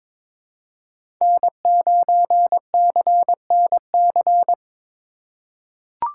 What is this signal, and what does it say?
Morse code, heard as a keyed tone, about 22 wpm, 700 Hz. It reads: N9CNC
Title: USA Call Signs - 5 Characters - 2x Character Spacing - 22wpm